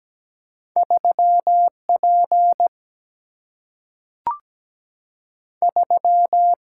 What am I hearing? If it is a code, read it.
Morse code, heard as a keyed tone, about 17 wpm, 700 Hz. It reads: 3 P 3
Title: Single-Letter Single-Number 17wpm